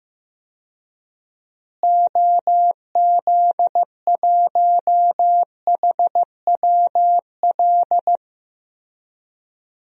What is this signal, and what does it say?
Morse code, heard as a keyed tone, about 15 wpm, 700 Hz. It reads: OZ1HWL
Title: DXCC-100 Call Signs - 6 Characters - Repeated 3x - 15wpm